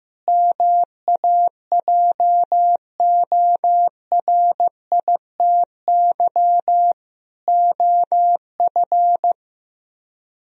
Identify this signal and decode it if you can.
Morse code, heard as a keyed tone, about 15 wpm, 700 Hz. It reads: MAJORITY OF
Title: Speed-Racing - Sets of 4 Words 15wpm